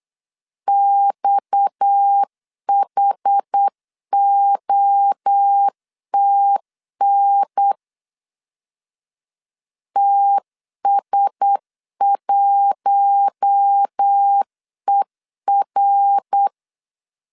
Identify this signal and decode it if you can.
Morse code, heard as a keyed tone, about 8 wpm, 800 Hz. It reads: XHOTN TS1ER
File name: CW / DET / 1 6 1500C S C 8wpm 5219